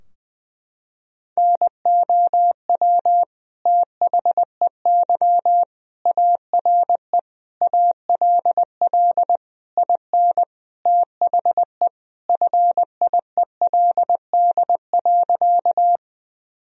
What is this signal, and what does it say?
Morse code, heard as a keyed tone, about 20 wpm, 700 Hz. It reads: NOW THEY ARE ALL IN THE FIELD.
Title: Sentences from Top 1000 Words - Review Part 2 - 20wpm